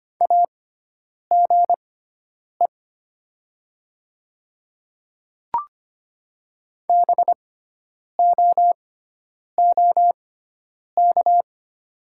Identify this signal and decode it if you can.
Morse code, heard as a keyed tone, about 25 wpm, spaced as for 9 wpm, 700 Hz. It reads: AGE BOOK
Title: 6x Character Spacing - Top 500 Words - 25wpm